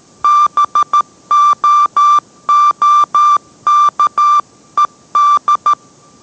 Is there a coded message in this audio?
Yes: Morse code from 0.2 to 5.7 s, heard as a keyed tone, about 13 wpm, 1180 Hz. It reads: BOOKED